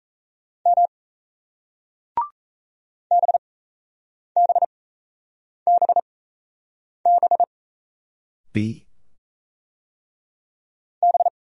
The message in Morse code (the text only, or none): M BBBB B